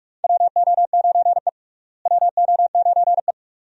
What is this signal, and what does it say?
Morse code, heard as a keyed tone, about 46 wpm, 700 Hz. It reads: WO9E WO9E
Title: USA Call Signs - 4 Characters - Repeated 2x - 45wpm